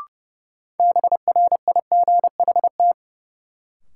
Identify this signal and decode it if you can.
Morse code, heard as a keyed tone, about 32 wpm, 700 Hz. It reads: BRIGHT
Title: Sentences from Top 1000 Words - New Word Drill - 30wpm